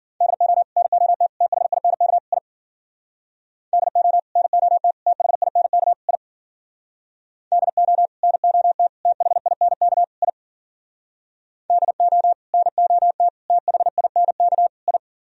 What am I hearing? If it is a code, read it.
Morse code, heard as a keyed tone, about 49 wpm, 700 Hz. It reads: DO NOT THINK I DO NOT THINK I DO NOT THINK I DO NOT THINK I
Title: Speed-Racing - Sets of 4 Words 40wpm